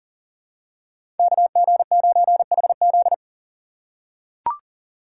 Morse code, heard as a keyed tone, about 40 wpm, 700 Hz. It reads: KG9HZ
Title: USA Call Signs - 5 Characters - 2x Character Spacing - 40wpm